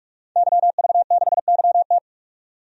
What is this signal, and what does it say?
Morse code, heard as a keyed tone, about 42 wpm, 700 Hz. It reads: YV6YT
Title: DXCC-100 Call Signs - 5 Characters - 45wpm